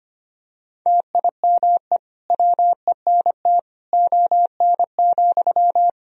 Morse code, heard as a keyed tone, about 25 wpm, 700 Hz. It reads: TIME WENT ON,
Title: Sentences from Top 1000 Words - Review Part 3 - 25wpm